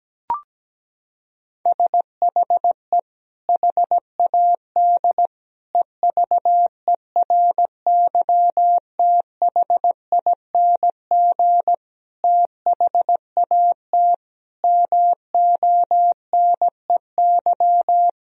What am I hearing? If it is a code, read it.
Morse code, heard as a keyed tone, about 17 wpm, 700 Hz. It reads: SHE HAD EVERYTHING THAT MONEY
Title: Sentences from Top 1000 Words - Review Part 4 - 17wpm